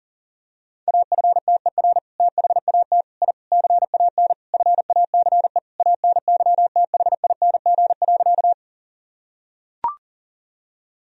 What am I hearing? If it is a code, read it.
Morse code, heard as a keyed tone, about 40 wpm, 700 Hz. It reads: AFTER THAT I CAN FACE ANYTHING.
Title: Sentences from Top 1000 Words - Review Part 2 - 40wpm